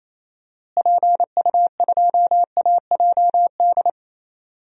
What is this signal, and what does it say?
Morse code, heard as a keyed tone, about 28 wpm, 700 Hz. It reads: PU2AJD
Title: DXCC-100 Call Signs - 6 Characters - Repeated 3x - 28wpm